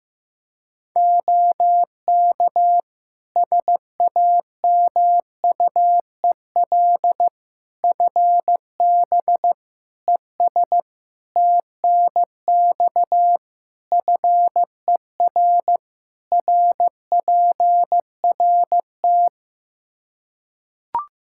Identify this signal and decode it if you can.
Morse code, heard as a keyed tone, about 15 wpm, 700 Hz. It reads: OK SAMUEL FB ES TNX FER RPRT
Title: Practice QSO - Exchange 2 - Info - 15wpm